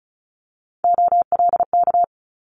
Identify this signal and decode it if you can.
Morse code, heard as a keyed tone, about 36 wpm, 700 Hz. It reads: OLK
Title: Multi-letter Phonemes Quick Fire 35wpm